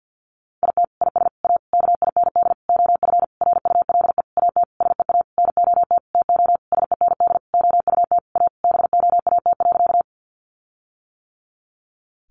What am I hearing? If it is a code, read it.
Morse code, heard as a keyed tone, about 50 wpm, 700 Hz. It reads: IT IS A KIND OF RULE AT SEA NOT TO SEND OUT A BOAT.
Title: Sentences from Top 1500 Words 50wpm